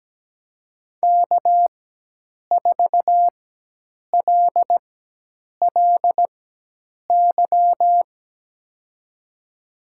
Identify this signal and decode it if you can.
Morse code, heard as a keyed tone, about 17 wpm, spaced as for 8 wpm, 700 Hz. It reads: K4LLY
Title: USA Call Signs - 5 Characters - 4x Character Spacing - 17wpm